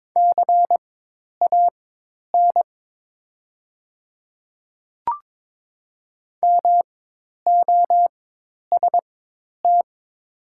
Morse code, heard as a keyed tone, about 22 wpm, spaced as for 10 wpm, 700 Hz. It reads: CAN MOST
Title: 4x Character Spacing - Top 200 Words - 22wpm